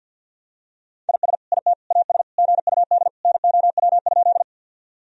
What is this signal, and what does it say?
Morse code, heard as a keyed tone, about 51 wpm, 700 Hz. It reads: IS IT AS GUD NOW?
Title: Sentences from Top 200 Words and Abbreviations 50wpm